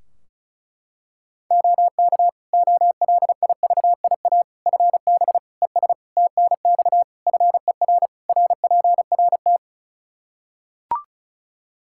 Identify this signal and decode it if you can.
Morse code, heard as a keyed tone, about 34 wpm, 700 Hz. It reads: OK OLIVIA FB ES TNX FER RPRT
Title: Practice QSO - Exchange 2 - Info - 35wpm